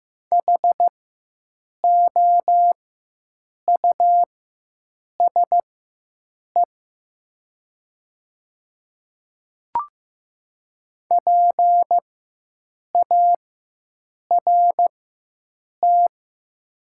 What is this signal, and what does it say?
Morse code, heard as a keyed tone, about 15 wpm, spaced as for 7 wpm, 700 Hz. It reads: HOUSE PART
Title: 4x Character Spacing - Top 200 Words - 15wpm